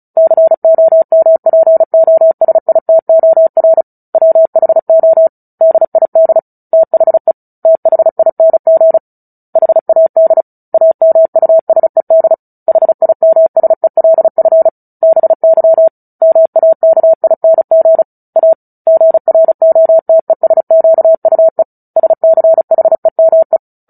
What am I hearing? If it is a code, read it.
Morse code, heard as a keyed tone, about 34 wpm, 650 Hz. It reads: COMPOSITOR WHO DID THE THING HAD AMUSED HIMSELF BY MAKING A GROTESQUE SCHEME
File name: Book Two / 09-Chapter 9 - 35 WPM